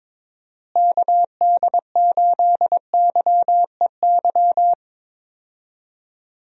Morse code, heard as a keyed tone, about 22 wpm, 700 Hz. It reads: KD8YEY
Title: USA Call Signs - 6 Characters - Repeated 2x - 22wpm